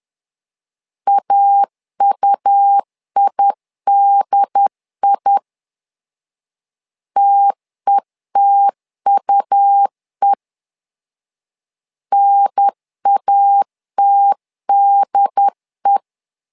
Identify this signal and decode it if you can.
Morse code, heard as a keyed tone, about 10 wpm, 800 Hz. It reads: AUIDI TETUE NATDE
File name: LEZIONI / 2 UHDN / 2 UHDN 10wpm 1904